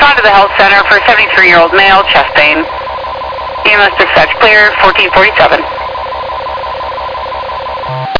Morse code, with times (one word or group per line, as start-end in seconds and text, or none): none